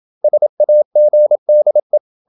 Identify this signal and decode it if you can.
Morse code, heard as a keyed tone, about 27 wpm, 600 Hz. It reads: SAGDE
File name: Ferie QRQ 27wpm